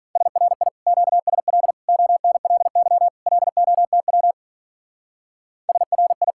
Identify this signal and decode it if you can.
Morse code, heard as a keyed tone, about 44 wpm, 700 Hz. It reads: SRI QSL ONLY LOTW SRI